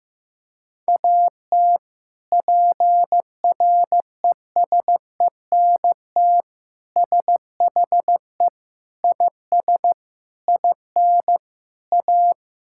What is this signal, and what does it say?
Morse code, heard as a keyed tone, about 15 wpm, 700 Hz. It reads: AT PRESENT SHE IS IN A
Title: Sentences from Top 1000 Words - Review Part 5 - 15wpm